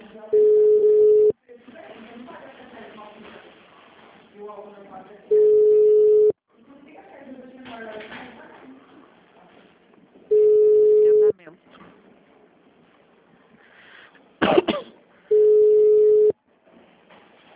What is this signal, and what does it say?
Ringback tone (Europe, 425 Hz), with speech.
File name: out- -2002- - - .142